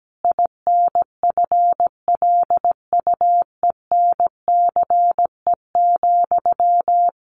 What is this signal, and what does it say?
Morse code, heard as a keyed tone, about 17 wpm, 700 Hz. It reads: INFLUENCE,
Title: Sentences from Top 1100 Words 17wpm